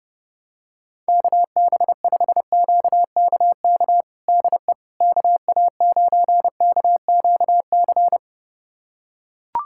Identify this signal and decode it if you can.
Morse code, heard as a keyed tone, about 29 wpm, 700 Hz. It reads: KB5QKK DE KA9KQC
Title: USA Callsign DE Callsign - 6 Characters - 30wpm